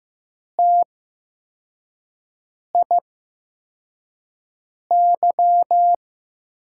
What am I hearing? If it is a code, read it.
Morse code, heard as a keyed tone, about 15 wpm, spaced as for 4 wpm, 700 Hz. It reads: TIY
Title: USA Call Signs - 8x Character Spacing - 15wpm